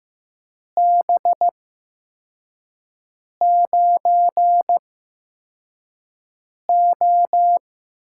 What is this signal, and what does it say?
Morse code, heard as a keyed tone, about 15 wpm, spaced as for 4 wpm, 700 Hz. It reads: B9O
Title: USA Call Signs - 8x Character Spacing - 15wpm